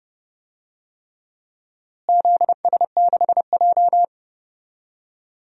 Morse code, heard as a keyed tone, about 31 wpm, 700 Hz. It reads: ZS6J
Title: DXCC-100 Call Signs - 4 Characters - Repeated 3x - 30wpm